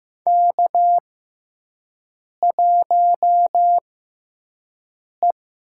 Morse code, heard as a keyed tone, about 15 wpm, spaced as for 5 wpm, 700 Hz. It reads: K1E
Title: USA Call Signs - 5 Characters - 6x Character Spacing - 15wpm